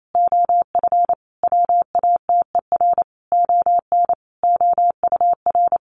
Morse code, heard as a keyed tone, about 28 wpm, 700 Hz. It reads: OF WATER ON OUR
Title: Sentences from Top 1400 Words 28wpm